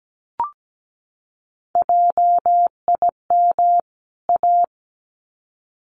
Morse code, heard as a keyed tone, about 17 wpm, 700 Hz. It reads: JIM A
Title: CWT Exchange 17wpm